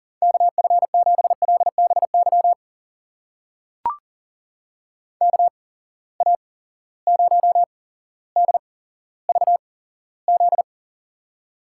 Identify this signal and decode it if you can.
Morse code, heard as a keyed tone, about 40 wpm, spaced as for 17 wpm, 700 Hz. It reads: KF7LBY KA0DVZ
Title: USA Call Signs - 6 Characters - 8x Character Spacing - 40wpm